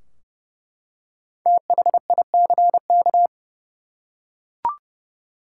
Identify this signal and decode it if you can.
Morse code, heard as a keyed tone, about 29 wpm, 700 Hz. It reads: THICK
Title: Five Letter Words - 30wpm